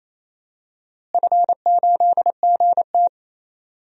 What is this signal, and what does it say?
Morse code, heard as a keyed tone, about 28 wpm, 700 Hz. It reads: F8GT
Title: DXCC-100 Call Signs - 4 Characters - Repeated 3x - 28wpm